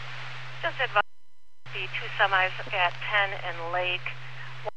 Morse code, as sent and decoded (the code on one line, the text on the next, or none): none